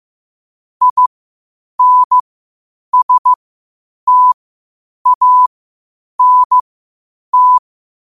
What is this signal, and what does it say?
Morse code, heard as a keyed tone, about 15 wpm, spaced as for 8 wpm, 1000 Hz. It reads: INSTANT
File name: qotdfiles / 8wpm